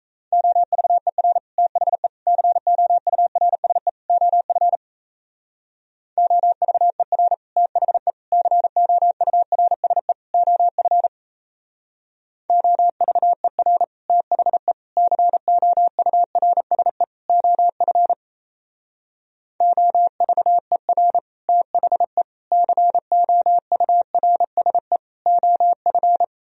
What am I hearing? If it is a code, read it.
Morse code, heard as a keyed tone, about 33 wpm, 700 Hz. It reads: OVER THE COURSE OF OVER THE COURSE OF OVER THE COURSE OF OVER THE COURSE OF
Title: Speed-Racing - Sets of 4 Words 28wpm